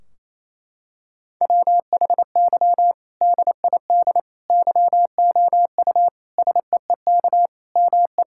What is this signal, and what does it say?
Morse code, heard as a keyed tone, about 28 wpm, 700 Hz. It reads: WHY DID YOU SEEK ME
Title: Sentences from Top 1600 Words 28wpm